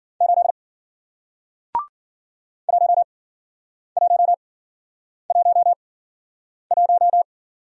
Morse code, heard as a keyed tone, about 54 wpm, spaced as for 9 wpm, 700 Hz. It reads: 8 1111